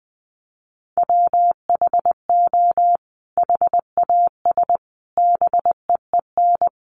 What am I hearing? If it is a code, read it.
Morse code, heard as a keyed tone, about 19 wpm, 700 Hz. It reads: WHO HAS BEEN
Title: Sentences from Top 1100 Words 20wpm